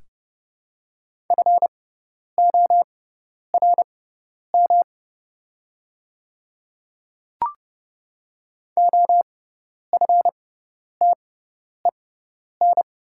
Morse code, heard as a keyed tone, about 30 wpm, spaced as for 10 wpm, 700 Hz. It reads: FORM OFTEN
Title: 6x Character Spacing - Top 300 Words - 30wpm